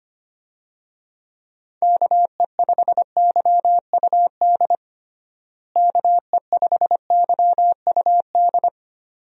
Morse code, heard as a keyed tone, about 25 wpm, 700 Hz. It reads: KE5YUD KE5YUD
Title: USA Call Signs - 6 Characters - Repeated 2x - 25wpm